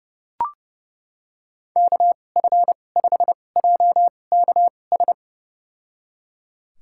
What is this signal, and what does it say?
Morse code, heard as a keyed tone, about 32 wpm, spaced as for 22 wpm, 700 Hz. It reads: KF5JKS
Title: USA Call Signs - 2x Character Spacing - 30wpm